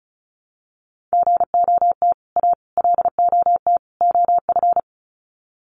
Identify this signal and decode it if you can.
Morse code, heard as a keyed tone, about 34 wpm, 700 Hz. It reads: GOT A LOT OF
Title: Sets of 4 Words 35wpm